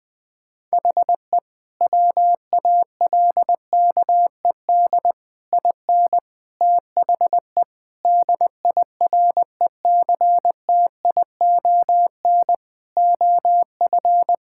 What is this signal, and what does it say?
Morse code, heard as a keyed tone, about 20 wpm, 700 Hz. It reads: HE WALKED IN THE DIRECTION OF
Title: Sentences with 5-Word Phrases - Spoken Beginning - 20wpm